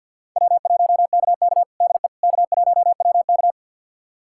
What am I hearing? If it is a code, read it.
Morse code, heard as a keyed tone, about 51 wpm, 700 Hz. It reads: W1KK DE K1WK
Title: USA Callsign DE Callsign - 4 Characters - 50wpm